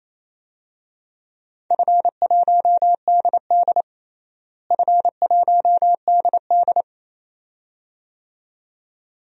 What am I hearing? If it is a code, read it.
Morse code, heard as a keyed tone, about 28 wpm, 700 Hz. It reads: F1DD F1DD
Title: DXCC-100 Call Signs - 4 Characters - Repeated 2x - 28wpm